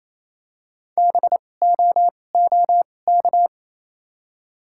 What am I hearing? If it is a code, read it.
Morse code, heard as a keyed tone, about 28 wpm, spaced as for 20 wpm, 700 Hz. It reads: BOOK
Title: 2x Character Spacing - Top 500 Words - 28wpm